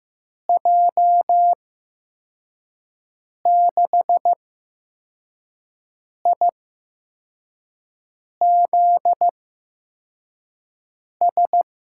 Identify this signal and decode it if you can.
Morse code, heard as a keyed tone, about 15 wpm, spaced as for 4 wpm, 700 Hz. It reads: J6IZS
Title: USA Call Signs - 6 Characters - 8x Character Spacing - 15wpm